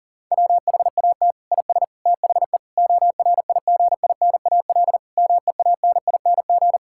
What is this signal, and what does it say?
Morse code, heard as a keyed tone, about 40 wpm, 700 Hz. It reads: WHAT IS THE ORIGINAL MEANING
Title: Sentences from Top 1000 Words - Review Part 6 - 40wpm